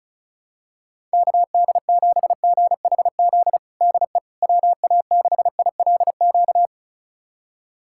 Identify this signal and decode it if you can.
Morse code, heard as a keyed tone, about 36 wpm, 700 Hz. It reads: KD7GHZ DE WA6ILQ
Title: USA Callsign DE Callsign - 6 Characters - 35wpm